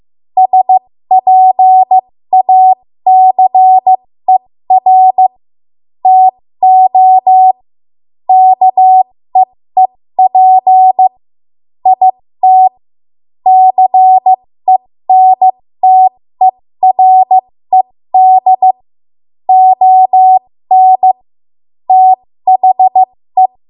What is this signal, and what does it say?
Morse code, heard as a keyed tone, about 15 wpm, spaced as for 13 wpm, 750 Hz. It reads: SPACER TO KEEP IT CENTERED ON THE
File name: Morse / 13WPM